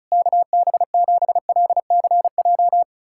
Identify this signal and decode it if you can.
Morse code, heard as a keyed tone, about 36 wpm, 700 Hz. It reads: KB7LCJ
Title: USA Callsign DE Callsign 35wpm